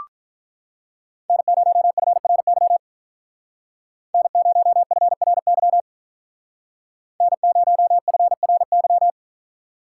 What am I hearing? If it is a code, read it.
Morse code, heard as a keyed tone, about 43 wpm, 700 Hz. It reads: N0FRY N0FRY N0FRY
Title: Speed Racing - USA 1x3 Call Signs 35wpm